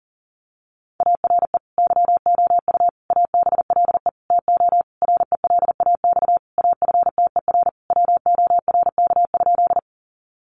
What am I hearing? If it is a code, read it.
Morse code, heard as a keyed tone, about 40 wpm, 700 Hz. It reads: ARE YOU ABLE TO RELAX AFTER WORK?
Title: Sentences from Top 1400 Words 40wpm